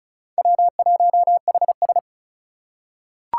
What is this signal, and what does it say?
Morse code, heard as a keyed tone, about 34 wpm, 700 Hz. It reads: W1HS
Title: USA Call Signs - 6x Character Spacing - 35wpm